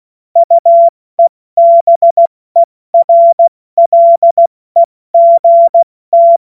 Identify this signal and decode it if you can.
Morse code, heard as a keyed tone, about 16 wpm, spaced as for 14 wpm, 680 Hz. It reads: UEBERLEGT
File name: Morsekurs / Witze / Witz 09. August 2022 70 F80